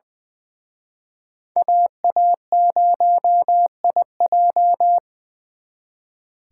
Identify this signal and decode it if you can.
Morse code, heard as a keyed tone, about 24 wpm, 700 Hz. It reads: AA0IJ
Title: Speed Racing - USA 2x2 Call Signs 20wpm